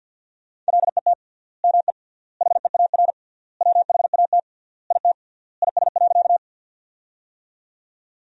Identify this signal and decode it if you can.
Morse code, heard as a keyed tone, about 50 wpm, 700 Hz. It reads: LET ME HEAR WHAT IT IS.
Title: 3x Word-Spacing - Sentences from Top 300 Words - 50wpm